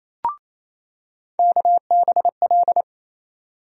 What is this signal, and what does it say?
Morse code, heard as a keyed tone, about 28 wpm, 700 Hz. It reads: KBL
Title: USA Call Sign Postfixes - 3 Letters 28wpm